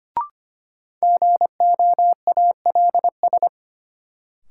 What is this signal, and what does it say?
Morse code, heard as a keyed tone, about 25 wpm, 700 Hz. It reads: GOALS
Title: Sentences from Top 1000 Words - Review Part 6 - New Word Drill - 25wpm